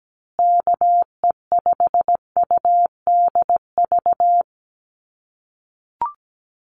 Morse code, heard as a keyed tone, about 17 wpm, 700 Hz. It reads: KE5UDV
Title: USA Call Signs Repeated 2x 17wpm